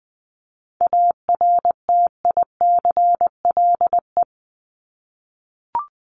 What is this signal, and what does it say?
Morse code, heard as a keyed tone, about 20 wpm, 700 Hz. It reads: ARTICLE
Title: Top 751-1000 Words 20wpm